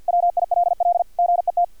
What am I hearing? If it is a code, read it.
Morse code, heard as a keyed tone, about 50 wpm, 700 Hz, with radio noise. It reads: WILL GET
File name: archived / CRSnet